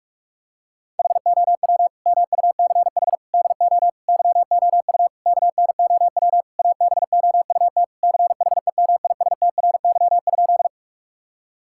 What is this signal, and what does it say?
Morse code, heard as a keyed tone, about 41 wpm, 690 Hz. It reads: HOW MUCH DO YOU KNOW ABOUT CHEMISTRY?
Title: Sentences from Top 1000 Words - Review Part 6 - 45wpm